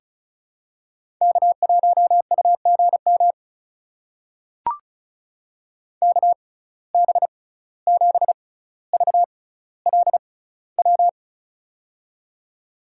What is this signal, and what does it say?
Morse code, heard as a keyed tone, about 36 wpm, spaced as for 17 wpm, 700 Hz. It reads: K1UGM KB7VLW
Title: USA Call Signs - 6x Character Spacing - 35wpm